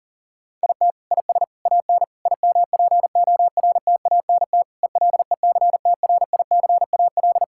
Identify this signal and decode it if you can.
Morse code, heard as a keyed tone, about 40 wpm, 690 Hz. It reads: IT IS AN IMPORTANT ELECTRICAL
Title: Sentences from Top 1000 Words - Review Part 6 - 40wpm